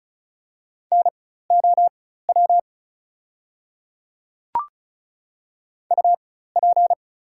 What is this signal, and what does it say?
Morse code, heard as a keyed tone, about 34 wpm, spaced as for 16 wpm, 700 Hz. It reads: NOW UP
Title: 4x Character Spacing - Top 100 Words - 35wpm